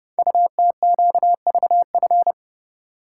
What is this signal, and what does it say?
Morse code, heard as a keyed tone, about 30 wpm, 700 Hz. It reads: UTQVF
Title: Code Groups - 5 Letters - 30wpm